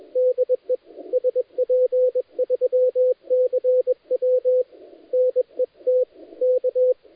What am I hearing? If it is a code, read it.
Morse code, heard as a keyed tone, about 21 wpm, 510 Hz. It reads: DE SP3CW NET K
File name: dlazaawansowanych / nagraniazpasma / titawaNETcontrol